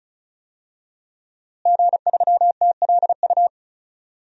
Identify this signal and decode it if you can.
Morse code, heard as a keyed tone, about 36 wpm, 700 Hz. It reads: G3TLU
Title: DXCC-100 Call Signs - 5 Characters - Repeated 3x - 35wpm